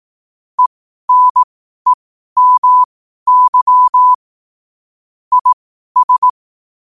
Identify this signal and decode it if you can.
Morse code, heard as a keyed tone, about 18 wpm, spaced as for 12 wpm, 1000 Hz. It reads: ENEMY IS
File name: qotdfiles / 12wpm